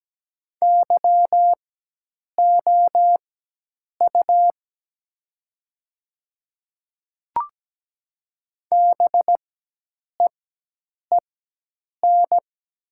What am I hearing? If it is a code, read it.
Morse code, heard as a keyed tone, about 17 wpm, spaced as for 8 wpm, 700 Hz. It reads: YOU BEEN
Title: 4x Character Spacing - Top 200 Words - 17wpm